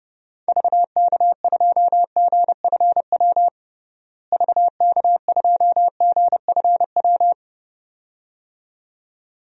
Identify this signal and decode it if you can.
Morse code, heard as a keyed tone, about 32 wpm, 700 Hz. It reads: VK2GFW VK2GFW
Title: DXCC-100 Call Signs - 6 Characters - Repeated 2x - 30wpm